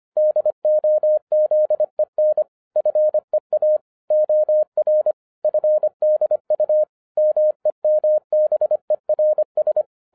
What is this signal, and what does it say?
Morse code, heard as a keyed tone, about 24 wpm, 610 Hz. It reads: DOZEN FEA OR FDU MEMBERS
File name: 2012-01-mp3 / n14-FUN WITH NUMBERS-25wpm